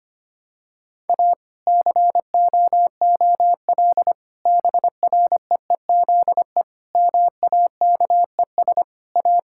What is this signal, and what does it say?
Morse code, heard as a keyed tone, about 25 wpm, 700 Hz. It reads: A COOL BREEZE MAKES A
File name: practice / 25wpm / Difficult Sentences with Hints - 25wpm